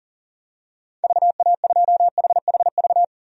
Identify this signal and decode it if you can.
Morse code, heard as a keyed tone, about 40 wpm, 700 Hz. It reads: VA2HHV
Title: DXCC-100 Call Signs - 6 Characters - 40wpm